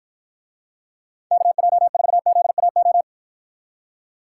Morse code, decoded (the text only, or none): KJ4ZAO